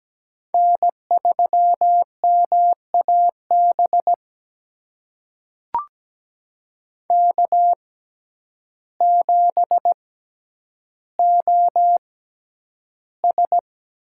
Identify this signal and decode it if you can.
Morse code, heard as a keyed tone, about 17 wpm, spaced as for 9 wpm, 700 Hz. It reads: N3MAB K7OS
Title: USA Call Signs - 5 Characters - 6x Character Spacing - 17wpm